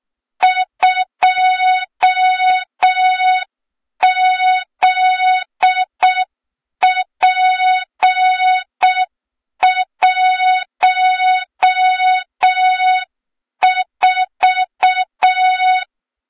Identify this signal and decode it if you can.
Morse code, heard as a keyed tone, about 6 wpm, 3700 Hz. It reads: 2ZP14
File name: Morze-HA2MN / hang / 23 GYAKORLAT